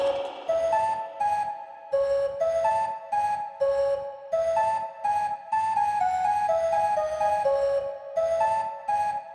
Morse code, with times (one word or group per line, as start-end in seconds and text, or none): none